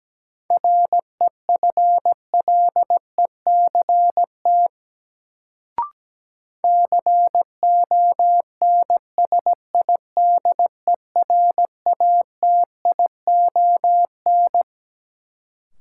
Morse code, heard as a keyed tone, about 17 wpm, 700 Hz. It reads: REFLECT CONSIDERATION
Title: Sentences from Top 1500 Words - New Word Drill Alt - 17wpm